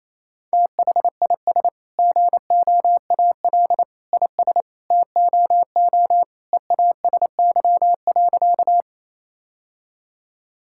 Morse code, heard as a keyed tone, about 27 wpm, 700 Hz. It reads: THIS GOAL IS TOO EASY.
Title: Sentences from Top 1000 Words - Review Part 5 - 28wpm